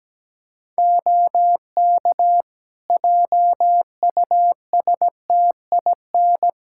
Morse code, heard as a keyed tone, about 17 wpm, 700 Hz. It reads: OK JUSTIN
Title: Practice QSO - Exchange 3 - More Info - 17wpm